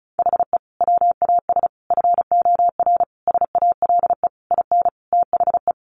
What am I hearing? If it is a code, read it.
Morse code, heard as a keyed tone, about 34 wpm, 700 Hz. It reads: HE WAS FOR SALE IN THE
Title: Sentences from Top 1400 Words 35wpm